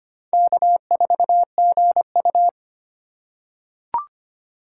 Morse code, heard as a keyed tone, about 25 wpm, 700 Hz. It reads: K4GU
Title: USA Call Signs - 4 Characters - 3x Character Spacing - 25wpm